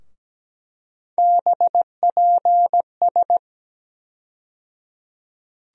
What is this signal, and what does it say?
Morse code, heard as a keyed tone, about 17 wpm, 700 Hz. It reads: BPS
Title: USA Call Sign Postfixes - 3 Letters 17wpm